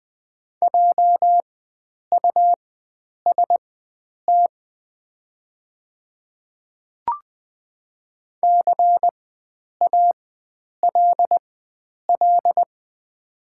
Morse code, heard as a keyed tone, about 20 wpm, spaced as for 9 wpm, 700 Hz. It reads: JUST CALL
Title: 4x Character Spacing - Top 300 Words - 20wpm